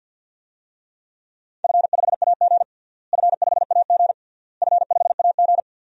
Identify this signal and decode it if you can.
Morse code, heard as a keyed tone, about 46 wpm, 700 Hz. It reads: F5AG F5AG F5AG
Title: DXCC-100 Call Signs - 4 Characters - Repeated 3x - 50wpm